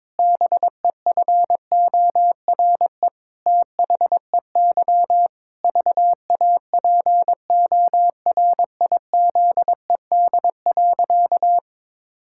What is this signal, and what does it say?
Morse code, heard as a keyed tone, about 22 wpm, 700 Hz. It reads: BEFORE THEY VAPORIZED.
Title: Pangram Challenge v2 - 22wpm